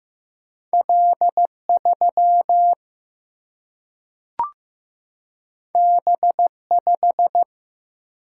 Morse code, heard as a keyed tone, about 15 wpm, 700 Hz. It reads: L 3 B 5
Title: Single-Letter Single-Number 15wpm